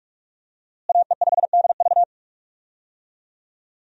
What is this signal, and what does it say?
Morse code, heard as a keyed tone, about 46 wpm, 700 Hz. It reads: AE5DV
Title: USA Call Signs - 5 Characters - Repeated 2x - 45wpm